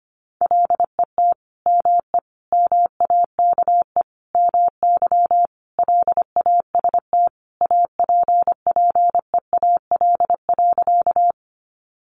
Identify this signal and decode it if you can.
Morse code, heard as a keyed tone, about 25 wpm, 700 Hz. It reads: LET ME MAKE MY LAST APPEAL.
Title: Sentences from Top 1300 Words 25wpm